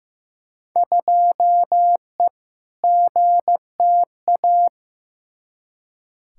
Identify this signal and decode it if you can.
Morse code, heard as a keyed tone, about 15 wpm, 700 Hz. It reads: 2E GTA
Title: Field Day Exchanges Encore - 15wpm